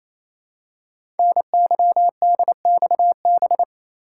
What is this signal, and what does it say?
Morse code, heard as a keyed tone, about 27 wpm, 700 Hz. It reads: NYDXB
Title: Code Groups - 5 Letters - 28wpm